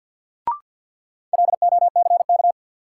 Morse code, heard as a keyed tone, about 52 wpm, 700 Hz. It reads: LOCK
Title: Sentences from Top 1300 Words - New Word Drill Alt - 50wpm